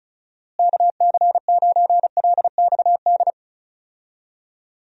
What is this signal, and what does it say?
Morse code, heard as a keyed tone, about 34 wpm, 700 Hz. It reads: KC9LXD
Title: USA Call Signs - 6 Characters - Repeated 3x - 35wpm